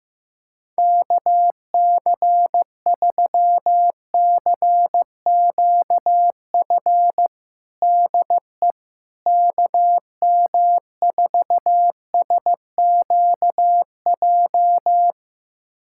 Morse code, heard as a keyed tone, about 16 wpm, 700 Hz. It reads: KC3CQF DE KM4SQJ
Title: USA Callsign DE Callsign - 6 Characters - 15wpm